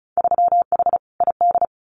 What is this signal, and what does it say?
Morse code, heard as a keyed tone, about 36 wpm, 700 Hz. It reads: 3H ID
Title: Winter Field Day Exchanges Encore 35wpm